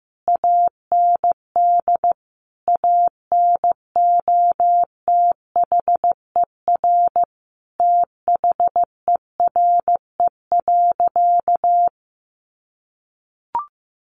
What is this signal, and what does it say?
Morse code, heard as a keyed tone, about 15 wpm, 700 Hz. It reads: AND ANOTHER THERE.
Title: Sentences from Top 1400 Words 15wpm